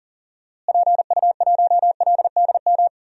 Morse code, heard as a keyed tone, about 40 wpm, 700 Hz. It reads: PU1LDM
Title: DXCC-100 Call Signs - 6 Characters - Repeated 3x - 40wpm